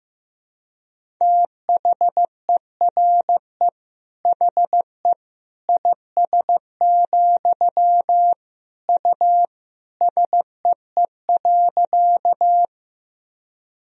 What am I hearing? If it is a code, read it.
Morse code, heard as a keyed tone, about 15 wpm, 700 Hz. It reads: THERE HE IS, U SEE.
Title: Sentences from Top 100 Words and Abbreviations 15wpm